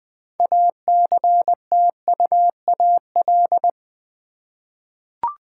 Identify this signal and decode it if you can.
Morse code, heard as a keyed tone, about 20 wpm, 700 Hz. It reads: ACTUAL
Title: 6 Letter Words - First 3 Letters Given - 20wpm